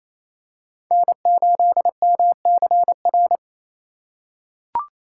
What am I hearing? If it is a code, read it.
Morse code, heard as a keyed tone, about 28 wpm, 700 Hz. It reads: N8MCR
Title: USA Call Signs - 5 Characters - 2x Character Spacing - 28wpm